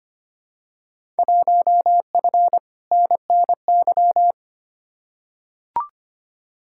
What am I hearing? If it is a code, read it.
Morse code, heard as a keyed tone, about 25 wpm, 700 Hz. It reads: 1F NNY
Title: Field Day Exchanges - 25wpm